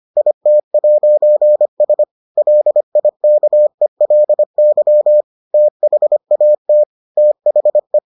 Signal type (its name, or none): Morse code